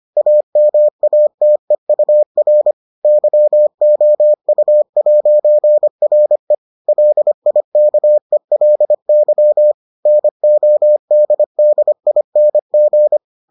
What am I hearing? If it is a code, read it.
Morse code, heard as a keyed tone, about 25 wpm, 600 Hz. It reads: AMATEUR YOU'RE LIKELY NODDING